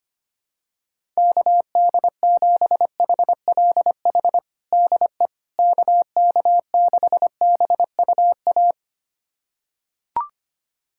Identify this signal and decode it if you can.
Morse code, heard as a keyed tone, about 25 wpm, 700 Hz. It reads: KD7HLH DE KK6BUA
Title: USA Callsign DE Callsign - 6 Characters - 25wpm